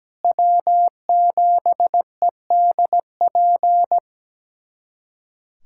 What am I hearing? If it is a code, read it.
Morse code, heard as a keyed tone, about 17 wpm, 700 Hz. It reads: W7EDP
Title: USA Callsign DE Callsign 17wpm